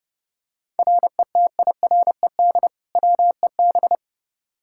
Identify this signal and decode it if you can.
Morse code, heard as a keyed tone, about 32 wpm, 700 Hz. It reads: RETIRED WEB
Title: K1USN SST QSO Practice 30wpm